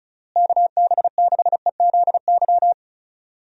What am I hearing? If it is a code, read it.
Morse code, heard as a keyed tone, about 36 wpm, 700 Hz. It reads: KB6EZY
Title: USA Call Signs - 6 Characters - Repeated 3x - 35wpm